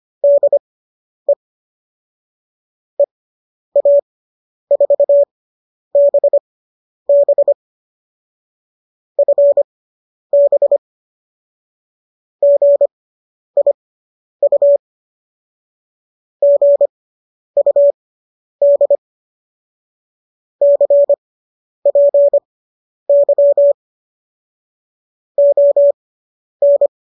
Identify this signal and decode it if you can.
Morse code, heard as a keyed tone, about 25 wpm, spaced as for 10 wpm, 570 Hz. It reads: DE EA4BB FB GIU GUD CPY ON